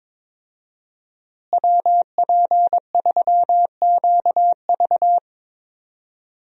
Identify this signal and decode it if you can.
Morse code, heard as a keyed tone, about 22 wpm, 700 Hz. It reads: WP3QV
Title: USA Call Signs - 5 Characters - Repeated 3x - 22wpm